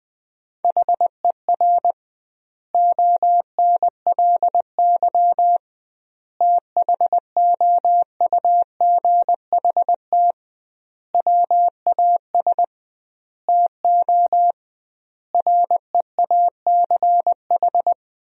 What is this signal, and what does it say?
Morse code, heard as a keyed tone, about 20 wpm, 700 Hz. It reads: HER ONLY THOUGHT WAS TO REACH
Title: 2x Word-Spacing - Sentences from Top 400 Words - 20wpm